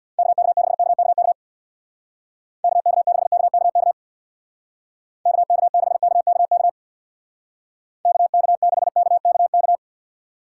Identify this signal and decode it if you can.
Morse code, heard as a keyed tone, about 59 wpm, 700 Hz. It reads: KK6KKK KK6KKK KK6KKK KK6KKK
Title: Speed Racing - USA 2x3 Call Signs 50wpm